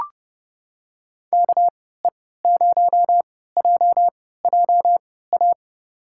Morse code, heard as a keyed tone, about 31 wpm, spaced as for 17 wpm, 700 Hz. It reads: KE0JJA
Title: USA Call Signs - 6 Characters - 3x Character Spacing - 30wpm